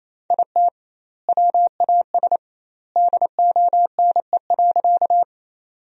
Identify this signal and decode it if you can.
Morse code, heard as a keyed tone, about 28 wpm, 700 Hz. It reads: IT WAS DONE.
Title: 2x Word-Spacing - Sentences from Top 500 Words - 28wpm